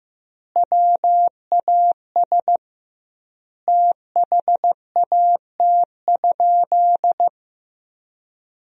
Morse code, heard as a keyed tone, about 16 wpm, 700 Hz. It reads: WAS THAT?
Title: 2x Word-Spacing - Sentences from Top 100 Words - 15wpm